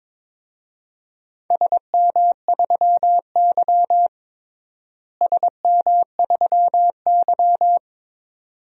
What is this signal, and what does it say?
Morse code, heard as a keyed tone, about 21 wpm, 700 Hz. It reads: SM3Y SM3Y
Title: DXCC-100 Call Signs - 4 Characters - Repeated 2x - 22wpm